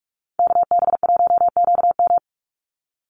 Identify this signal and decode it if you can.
Morse code, heard as a keyed tone, about 44 wpm, 700 Hz. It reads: KB1QM
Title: USA Call Signs 45wpm